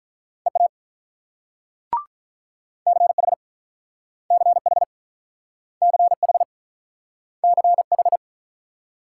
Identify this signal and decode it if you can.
Morse code, heard as a keyed tone, about 46 wpm, 700 Hz. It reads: EA CH CH CH CH